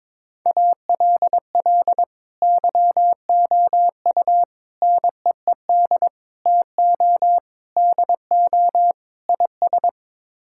Morse code, heard as a keyed tone, about 22 wpm, 700 Hz. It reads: ALL YOU NEED TO DO IS
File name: practice / 22wpm / Sets of 6 Words - 22wpm